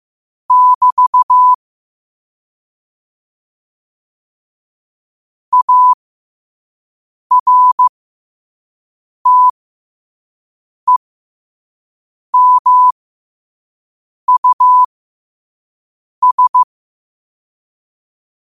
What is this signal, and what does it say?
Morse code, heard as a keyed tone, about 15 wpm, spaced as for 5 wpm, 1000 Hz. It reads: <BT> ARTEMUS
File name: qotdfiles / 5wpm